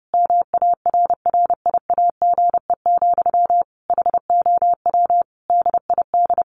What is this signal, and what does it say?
Morse code, heard as a keyed tone, about 28 wpm, 700 Hz. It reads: MARRIAGE, HOW DID
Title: Sentences from Top 1100 Words 30wpm